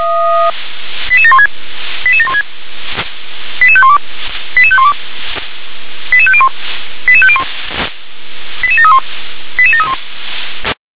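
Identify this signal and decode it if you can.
Five-tone selective call (ZVEI), heard as quick runs of five tones, tones of about 70 ms. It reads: 88716 88716 88512 88512 88501 88501 88731 88731
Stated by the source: ZVEI five-tone selective call received on SDR.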